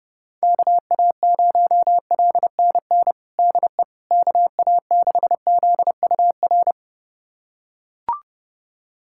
Morse code, heard as a keyed tone, about 30 wpm, 700 Hz. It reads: KA0LNN DE KA6ZUR
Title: USA Callsign DE Callsign - 6 Characters - 30wpm